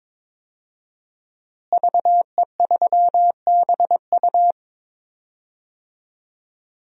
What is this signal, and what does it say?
Morse code, heard as a keyed tone, about 22 wpm, 700 Hz. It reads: VE3BU
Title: DXCC-100 Call Signs - 5 Characters - Repeated 3x - 22wpm